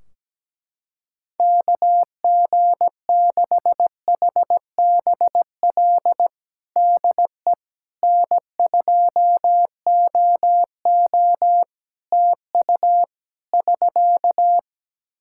Morse code, heard as a keyed tone, about 17 wpm, 700 Hz. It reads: KG6HBL DE N2OO TU <SK>
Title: K1USN SST QSO Practice 17wpm